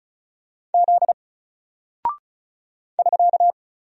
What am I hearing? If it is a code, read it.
Morse code, heard as a keyed tone, about 36 wpm, 700 Hz. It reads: Z <SK>
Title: Single Character (Letter:Number:Punctuation:Prosign) Alt 35wpm